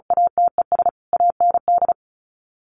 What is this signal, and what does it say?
Morse code, heard as a keyed tone, about 34 wpm, 700 Hz. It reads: ATES AND
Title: Sets of 4 Words 35wpm